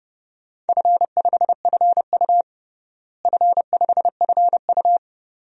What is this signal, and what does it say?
Morse code, heard as a keyed tone, about 30 wpm, 700 Hz. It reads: F5FU F5FU
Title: DXCC-100 Call Signs - 4 Characters - Repeated 2x - 30wpm